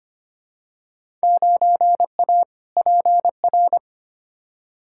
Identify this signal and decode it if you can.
Morse code, heard as a keyed tone, about 25 wpm, 700 Hz. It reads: 9A PR
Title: Field Day Exchanges Encore - 25wpm